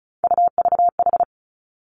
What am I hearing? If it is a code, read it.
Morse code, heard as a keyed tone, about 36 wpm, 700 Hz. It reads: UVH
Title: Three Challenging Letter-Number Combinations 35wpm